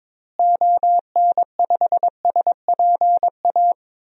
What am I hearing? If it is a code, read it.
Morse code, heard as a keyed tone, about 22 wpm, 700 Hz. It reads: ON5SPA
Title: DXCC-100 Call Signs - 6 Characters - Repeated 2x - 22wpm